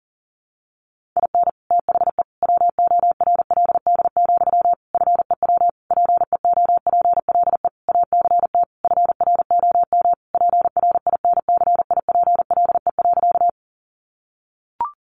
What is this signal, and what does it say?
Morse code, heard as a keyed tone, about 40 wpm, 700 Hz. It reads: IN THE WORLD, FEW PEOPLE ACT FROM PRINCIPLE.
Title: Sentences from Top 1400 Words 40wpm